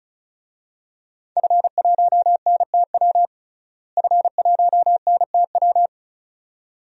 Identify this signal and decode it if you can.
Morse code, heard as a keyed tone, about 34 wpm, 700 Hz. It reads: F1NTW F1NTW
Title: DXCC-100 Call Signs - 5 Characters - Repeated 2x - 35wpm